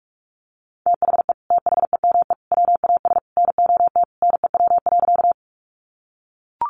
Morse code, heard as a keyed tone, about 44 wpm, 700 Hz. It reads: THE THEME WAS NOT NEW.
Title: Sentences from Top 1100 Words 45wpm